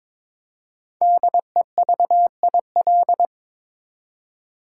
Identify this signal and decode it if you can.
Morse code, heard as a keyed tone, about 22 wpm, 690 Hz. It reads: DEVIL
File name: 5 Letter Words - First 3 Letters Given - 22wpm